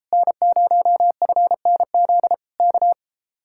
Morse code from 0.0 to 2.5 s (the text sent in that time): N0FNZ